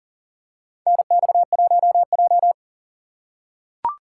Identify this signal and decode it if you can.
Morse code, heard as a keyed tone, about 38 wpm, 700 Hz. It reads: NX1J
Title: USA Call Signs - 2x Character Spacing - 40wpm